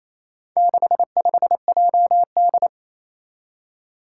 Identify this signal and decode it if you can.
Morse code, heard as a keyed tone, about 28 wpm, 700 Hz. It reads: 65JD
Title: Four Challenging Letter-Number Combinations 28wpm